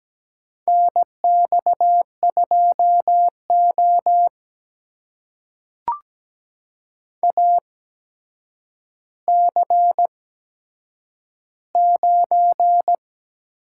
Morse code, heard as a keyed tone, about 17 wpm, spaced as for 8 wpm, 700 Hz. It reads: NX2O AC9
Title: USA Call Signs - 4 Characters - 8x Character Spacing - 17wpm